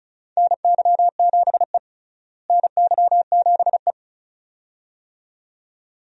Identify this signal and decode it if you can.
Morse code, heard as a keyed tone, about 36 wpm, 700 Hz. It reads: NY7E NY7E
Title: USA Call Signs - 4 Characters - Repeated 2x - 35wpm